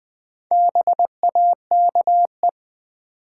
Morse code, heard as a keyed tone, about 21 wpm, 700 Hz. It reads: BAKE
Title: Four Letter Words - 20wpm